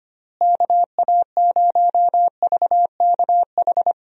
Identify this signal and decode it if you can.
Morse code, heard as a keyed tone, about 25 wpm, 700 Hz. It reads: KA0VKH
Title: USA Call Signs - 6 Characters - Repeated 3x - 25wpm